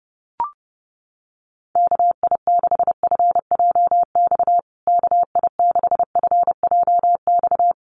Morse code, heard as a keyed tone, about 30 wpm, 700 Hz. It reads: KI6FJX KI6FJX
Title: USA Call Signs Repeated 3x 30wpm